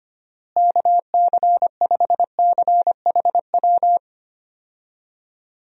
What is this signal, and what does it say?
Morse code, heard as a keyed tone, about 25 wpm, 700 Hz. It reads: KC5CHW
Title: USA Call Signs - 6 Characters - Repeated 3x - 25wpm